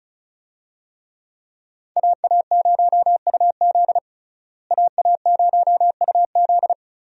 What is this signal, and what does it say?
Morse code, heard as a keyed tone, about 34 wpm, 700 Hz. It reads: AA0UZ AA0UZ
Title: USA Call Signs - 5 Characters - Repeated 2x - 35wpm